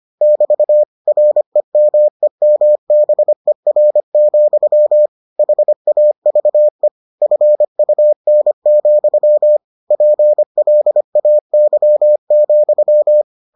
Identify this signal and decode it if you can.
Morse code, heard as a keyed tone, about 25 wpm, 600 Hz. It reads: <BT> REMEMBER, HAVE FUN, PLAY,